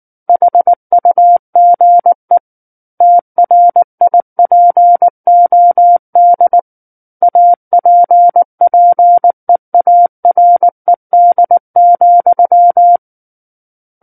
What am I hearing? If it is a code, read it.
Morse code, heard as a keyed tone, about 19 wpm, 700 Hz. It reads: HUGE TRIPOD APPEARED,
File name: Book One / 10-Chapter 10 - 19 WPM